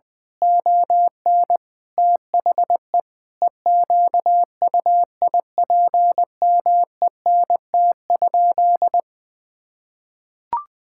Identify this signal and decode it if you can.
Morse code, heard as a keyed tone, about 20 wpm, 700 Hz. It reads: ON THE EQUIPMENT?
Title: Sentences from Top 1000 Words - Review Part 5 - 20wpm